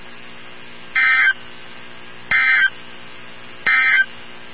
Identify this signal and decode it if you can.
Emergency Alert System (SAME) tones, heard as harsh buzzing bursts. It is heard from 0.8 to 4.1 s.
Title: NOAA Weather Radio 162.550